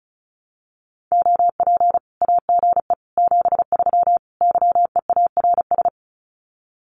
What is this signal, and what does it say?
Morse code, heard as a keyed tone, about 34 wpm, 700 Hz. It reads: OP AGE 73 YEARS
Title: Common QSO Phrases 35wpm